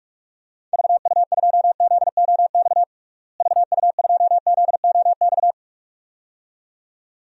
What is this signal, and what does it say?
Morse code, heard as a keyed tone, about 42 wpm, 700 Hz. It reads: VU2ZOX VU2ZOX
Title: DXCC-100 Call Signs - 6 Characters - Repeated 2x - 45wpm